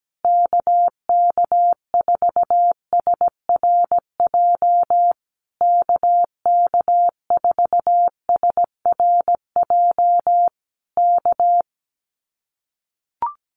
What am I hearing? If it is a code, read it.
Morse code, heard as a keyed tone, about 17 wpm, 700 Hz. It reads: KK4SRJ KK4SRJ K
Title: Common QSO Phrases Encore 17wpm